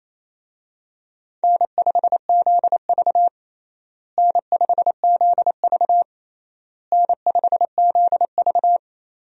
Morse code, heard as a keyed tone, about 28 wpm, 700 Hz. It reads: N5ZV N5ZV N5ZV
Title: USA Call Signs - 4 Characters - Repeated 3x - 28wpm